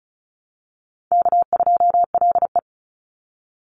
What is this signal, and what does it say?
Morse code, heard as a keyed tone, about 34 wpm, 700 Hz. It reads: K2LE
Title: Easy Call Signs Repeated 3x - 35wpm